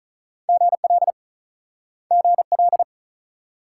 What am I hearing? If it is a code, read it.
Morse code, heard as a keyed tone, about 48 wpm, 700 Hz. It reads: GL GL